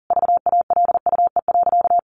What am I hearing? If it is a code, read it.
Morse code, heard as a keyed tone, about 40 wpm, 700 Hz. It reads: VALUE.
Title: Sentences from Top 1400 Words 40wpm